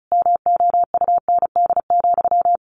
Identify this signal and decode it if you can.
Morse code, heard as a keyed tone, about 34 wpm, 700 Hz. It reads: MOUND,
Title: A Sound Of Thunder 35wpm